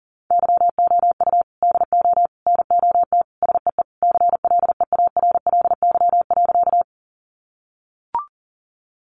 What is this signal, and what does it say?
Morse code, heard as a keyed tone, about 37 wpm, 700 Hz. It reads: YOU DO NOT SEE CLEARLY.
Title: Sentences from Top 1300 Words 40wpm